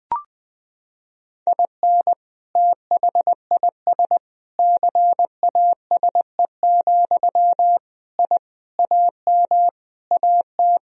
Morse code, heard as a keyed tone, about 20 wpm, 700 Hz. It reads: IN THIS CASE, I AM AT
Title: Sentences from Top 1000 Words - Review Part 4 - 20wpm